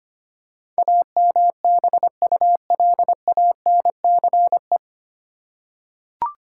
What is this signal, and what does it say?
Morse code, heard as a keyed tone, about 25 wpm, 700 Hz. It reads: AMBULANCE
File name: Nine Letter Words - 25wpm